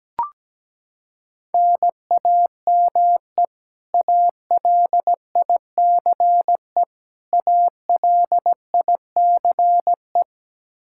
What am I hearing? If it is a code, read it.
Morse code, heard as a keyed tone, about 17 wpm, 700 Hz. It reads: NAME ALICE ALICE
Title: Practice QSO - Exchange 1 - Beginning - 17wpm